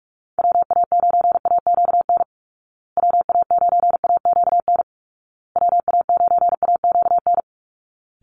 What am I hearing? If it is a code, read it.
Morse code, heard as a keyed tone, about 44 wpm, 700 Hz. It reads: WA9AQN WA9AQN WA9AQN
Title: Call Signs Repeated 2x - 45wpm